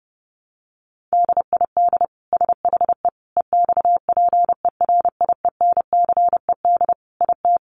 Morse code, heard as a keyed tone, about 30 wpm, 700 Hz. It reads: DID SHE EXPERIENCED IT
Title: Sentences from Top 1100 Words 30wpm